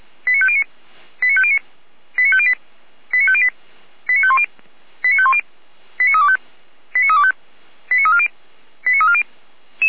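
Five-tone selective call (ZVEI), heard as quick runs of five tones, tones of about 80 ms. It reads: 89509 89509 89508 89508 89510 89510 89325 89325 89340 89340
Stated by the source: Five-tone ZVEI selective call received by SDR, fire brigade (Feuerwehr) Orlamünde.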